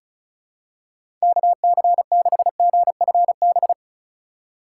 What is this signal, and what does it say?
Morse code, heard as a keyed tone, about 36 wpm, 700 Hz. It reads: KC6GFB
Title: USA Call Signs - 6 Characters - 4x Character Spacing - 35wpm